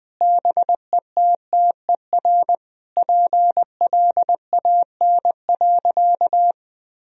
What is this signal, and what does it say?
Morse code, heard as a keyed tone, about 20 wpm, 700 Hz. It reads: BETTER PLAN.
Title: Sentences from Top 1000 Words - Review Part 2 - 20wpm